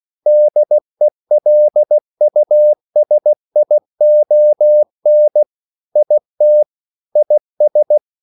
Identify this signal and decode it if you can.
Morse code, heard as a keyed tone, about 16 wpm, 600 Hz. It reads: DELUSION IT IS
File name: CW / Practice / EnglishQuotations0116